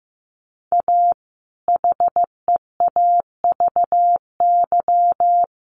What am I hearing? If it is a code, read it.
Morse code, heard as a keyed tone, about 15 wpm, 700 Hz. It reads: A HEAVY
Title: Sentences from Top 1100 Words 15wpm